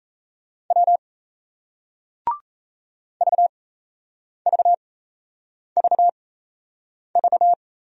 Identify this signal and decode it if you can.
Morse code, heard as a keyed tone, about 38 wpm, spaced as for 8 wpm, 700 Hz. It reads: W VVVV